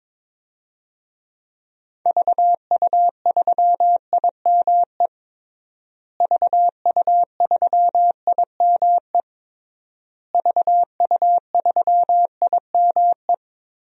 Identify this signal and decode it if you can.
Morse code, heard as a keyed tone, about 22 wpm, 700 Hz. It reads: VU3IME VU3IME VU3IME
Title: DXCC-100 Call Signs - 6 Characters - Repeated 3x - 22wpm